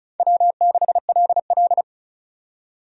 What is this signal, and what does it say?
Morse code, heard as a keyed tone, about 34 wpm, 700 Hz. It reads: W6LL
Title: USA Call Signs - 4 Characters - 35wpm